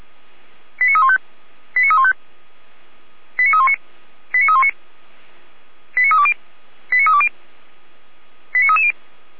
Five-tone selective call (ZVEI), heard as quick runs of five tones, tones of about 70 ms. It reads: 89315 89315 89319 89319 89320 89320 89330